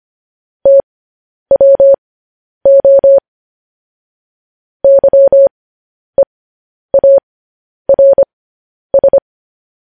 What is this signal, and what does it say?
Morse code, heard as a keyed tone, about 25 wpm, spaced as for 10 wpm, 550 Hz. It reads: TWO YEARS
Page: News Headlines in Morse Code at 10 WPM